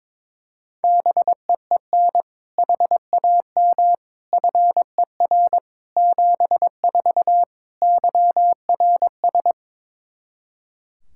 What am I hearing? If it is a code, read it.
Morse code, heard as a keyed tone, about 22 wpm, 700 Hz. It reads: BEEN HAM FER 74 YRS
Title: K1USN SST QSO Practice 22wpm